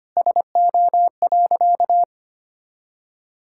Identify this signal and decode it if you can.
Morse code, heard as a keyed tone, about 25 wpm, 700 Hz. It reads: SO.
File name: practice / 25wpm / Sentences with 3-Word Phrases - 25wpm